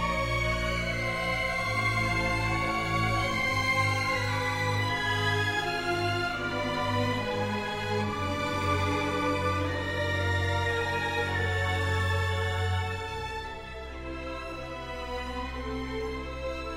Dial-up modem handshake.